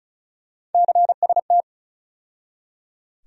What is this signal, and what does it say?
Morse code, heard as a keyed tone, about 36 wpm, 700 Hz. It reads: CST
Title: USA Call Sign Postfixes - 3 Letters 35wpm